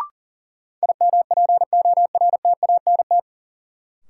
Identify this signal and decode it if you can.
Morse code, heard as a keyed tone, about 40 wpm, 700 Hz. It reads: IMPORTANT
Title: Sets of 3 Words - New Word Drill - 40wpm